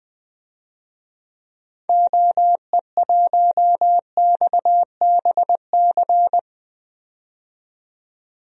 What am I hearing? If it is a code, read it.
Morse code, heard as a keyed tone, about 21 wpm, 700 Hz. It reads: OE1XBC
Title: DXCC-100 Call Signs - 6 Characters - 20wpm